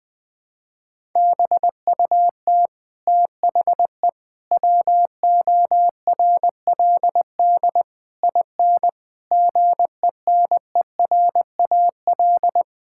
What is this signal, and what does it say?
Morse code, heard as a keyed tone, about 19 wpm, 700 Hz. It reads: BUT THE WORLD IN GENERAL
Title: Sentences from Top 1000 Words - Review Part 6 - 20wpm